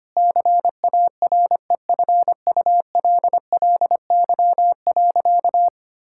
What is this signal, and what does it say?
Morse code, heard as a keyed tone, about 25 wpm, 700 Hz. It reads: CAREFULLY.
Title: Sentences from Top 1600 Words 25wpm